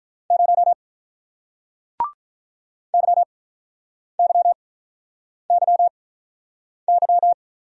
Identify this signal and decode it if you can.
Morse code, heard as a keyed tone, about 47 wpm, spaced as for 9 wpm, 700 Hz. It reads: 0 YYYY